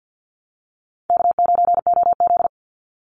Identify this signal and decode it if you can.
Morse code, heard as a keyed tone, about 50 wpm, 700 Hz. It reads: K9OZ
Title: NA SOTA Chaser Callsigns 50wpm